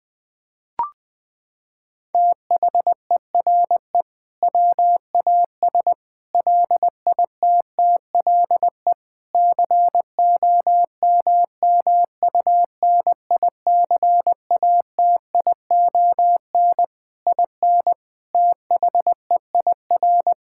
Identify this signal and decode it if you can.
Morse code, heard as a keyed tone, about 21 wpm, 700 Hz. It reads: THERE WAS LITTLE COMMUNICATION IN THEIR
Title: Sentences from Top 1000 Words - Review Part 6 - 20wpm